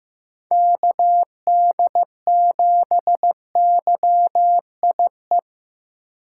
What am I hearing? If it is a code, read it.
Morse code, heard as a keyed tone, about 15 wpm, 700 Hz. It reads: KD7YIE
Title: USA Call Signs - 6 Characters - 8x Character Spacing - 15wpm